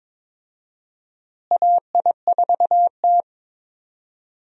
Morse code, heard as a keyed tone, about 22 wpm, 700 Hz. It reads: AI4T
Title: USA Call Signs - 4 Characters - Repeated 3x - 22wpm